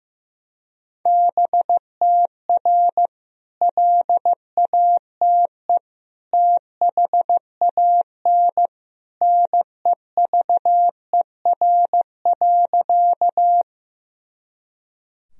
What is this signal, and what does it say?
Morse code, heard as a keyed tone, about 16 wpm, 700 Hz. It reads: BTR LATE THAN NEVER.
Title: Sentences from Top 400 Words and Abbreviations 15wpm